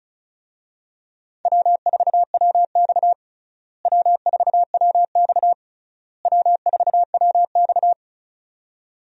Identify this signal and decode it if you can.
Morse code, heard as a keyed tone, about 36 wpm, 700 Hz. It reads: W4WX W4WX W4WX
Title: USA Call Signs - 4 Characters - Repeated 3x - 35wpm